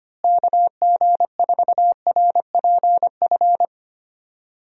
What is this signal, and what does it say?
Morse code, heard as a keyed tone, about 25 wpm, 700 Hz. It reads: KG4RPF
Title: USA Call Signs - 6 Characters - Repeated 3x - 25wpm